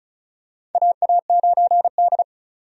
Morse code, heard as a keyed tone, about 35 wpm, 700 Hz. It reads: AA9D
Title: USA Call Signs - 4 Characters - 35wpm